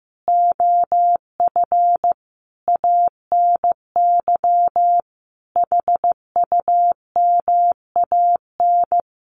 Morse code, heard as a keyed tone, about 15 wpm, 700 Hz. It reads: OF ANY HUMAN
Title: Sentences from Top 1100 Words 15wpm